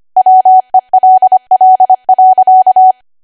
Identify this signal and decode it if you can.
Morse code, heard as a keyed tone, about 25 wpm, 750 Hz. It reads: WELL.
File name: Archive / 25 WPM / 25WPM